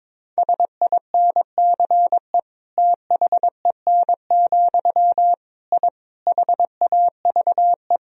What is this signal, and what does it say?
Morse code, heard as a keyed tone, about 22 wpm, 700 Hz. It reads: SINCE THEN, I HAVE
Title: Sentences from Top 1000 Words - Review Part 1 - 22wpm